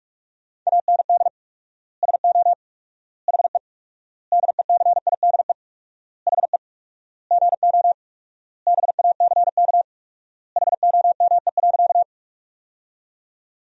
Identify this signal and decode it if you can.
Morse code, heard as a keyed tone, about 46 wpm, 700 Hz. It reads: AND SO HE DECIDE HE GO BACK HOME.
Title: 4x Word-Spacing - Sentences from Top 500 Words - 45wpm